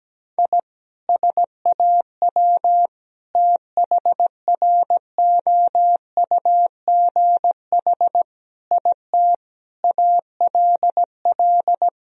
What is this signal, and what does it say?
Morse code, heard as a keyed tone, about 17 wpm, 700 Hz. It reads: I SAW THROUGH IT ALL
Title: Sentences from Top 1000 Words - Review Part 1 - 17wpm